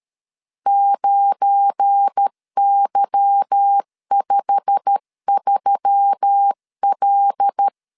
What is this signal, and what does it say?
Morse code, heard as a keyed tone, about 13 wpm, 800 Hz. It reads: 9Y53L
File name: LEZIONI / 10 SIGNS / 10B INT 12wpm 2414